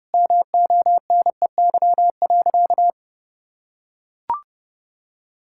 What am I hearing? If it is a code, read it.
Morse code, heard as a keyed tone, about 32 wpm, 700 Hz. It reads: MONEY.
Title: Sentences from Top 1000 Words - Review Part 5 - 30wpm